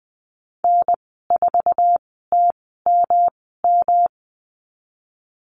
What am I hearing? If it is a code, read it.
Morse code, heard as a keyed tone, about 20 wpm, spaced as for 15 wpm, 700 Hz. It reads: N4TMM
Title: 173 Lesson 28g - Number 9 - Call Signs 20wpm